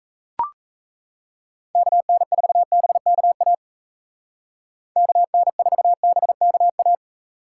Morse code, heard as a keyed tone, about 33 wpm, 700 Hz. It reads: KN4BKA KN4BKA
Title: Speed Racing - USA 2x3 Call Signs 28wpm